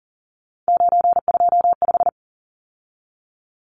Morse code, heard as a keyed tone, about 40 wpm, 700 Hz. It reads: 925
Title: Numbers - 3-Digit 40wpm